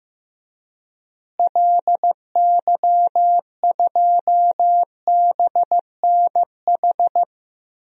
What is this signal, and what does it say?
Morse code, heard as a keyed tone, about 16 wpm, 700 Hz. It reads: LY2BNH
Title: DXCC-100 Call Signs - 6 Characters - Repeated 3x - 15wpm